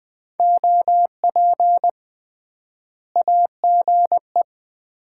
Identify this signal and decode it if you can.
Morse code, heard as a keyed tone, about 19 wpm, 700 Hz. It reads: OP AGE
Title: Common QSO Phrases - 3x Word-Spacing - 20wpm